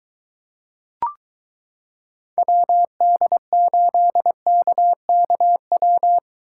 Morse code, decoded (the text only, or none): WD8KKW